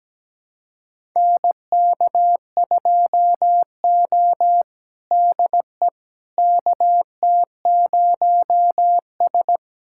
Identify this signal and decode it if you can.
Morse code, heard as a keyed tone, about 17 wpm, 700 Hz. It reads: NK2O DE KT0S
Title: USA Callsign DE Callsign - 4 Characters - 17wpm